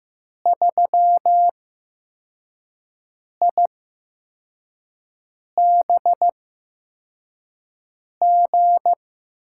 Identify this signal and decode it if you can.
Morse code, heard as a keyed tone, about 15 wpm, spaced as for 4 wpm, 700 Hz. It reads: 3IBG
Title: USA Call Signs - 5 Characters - 8x Character Spacing - 15wpm